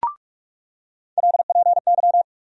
Morse code, heard as a keyed tone, about 42 wpm, 700 Hz. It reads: LPY
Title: USA Call Sign Postfixes - 3 Letters 45wpm